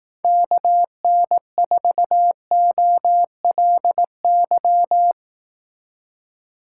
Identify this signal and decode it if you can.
Morse code, heard as a keyed tone, about 18 wpm, 700 Hz. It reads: KN4OLY
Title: Speed Racing - USA 2x3 Call Signs 15wpm